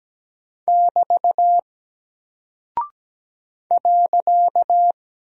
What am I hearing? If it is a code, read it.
Morse code, heard as a keyed tone, about 17 wpm, 700 Hz. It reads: <BT> .
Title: Single Character (Letter:Number:Punctuation:Prosign) Alt 17wpm